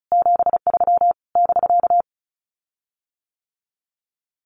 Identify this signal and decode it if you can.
Morse code, heard as a keyed tone, about 34 wpm, 700 Hz. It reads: 73 <BK>
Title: Common QSO Phrases Encore 35wpm